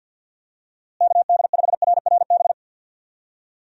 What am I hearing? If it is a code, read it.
Morse code, heard as a keyed tone, about 47 wpm, 700 Hz. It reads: KD5RRB
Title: USA Call Signs - 6 Characters - Repeated 2x - 50wpm